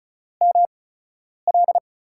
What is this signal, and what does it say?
Morse code, heard as a keyed tone, about 34 wpm, spaced as for 9 wpm, 700 Hz. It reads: ML
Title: USA Call Signs - 8x Character Spacing - 35wpm